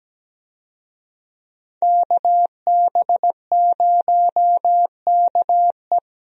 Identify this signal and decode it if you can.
Morse code, heard as a keyed tone, about 17 wpm, 700 Hz. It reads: KB0KE
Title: USA Call Signs - 5 Characters - 17wpm